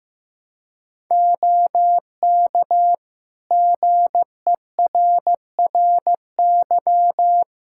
Morse code, heard as a keyed tone, about 15 wpm, 700 Hz. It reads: OK GERRY
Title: Practice QSO - Exchange 3 - More Info - 15wpm